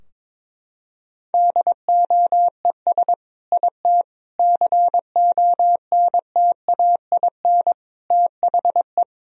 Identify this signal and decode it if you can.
Morse code, heard as a keyed tone, about 21 wpm, 700 Hz. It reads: DOES IT CONTAIN THE
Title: Sentences from Top 1000 Words - Review Part 2 - 22wpm